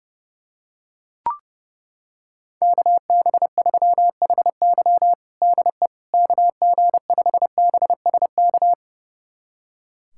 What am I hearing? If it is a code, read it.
Morse code, heard as a keyed tone, about 30 wpm, 700 Hz. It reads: KB3HY DE KG5BSK
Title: USA Callsign DE Callsign 30wpm